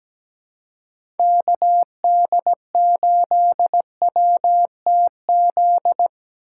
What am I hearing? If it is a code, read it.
Morse code, heard as a keyed tone, about 17 wpm, 700 Hz. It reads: KD8WTZ
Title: USA Call Signs - 2x Character Spacing - 17wpm